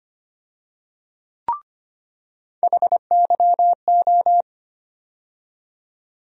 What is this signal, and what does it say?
Morse code, heard as a keyed tone, about 25 wpm, 700 Hz. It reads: HYO
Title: USA Call Sign Postfixes - 3 Letters 25wpm